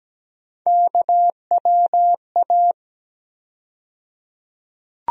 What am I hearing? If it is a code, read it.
Morse code, heard as a keyed tone, about 17 wpm, 700 Hz. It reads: KWA
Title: USA Call Sign Postfixes - 3 Letters 17wpm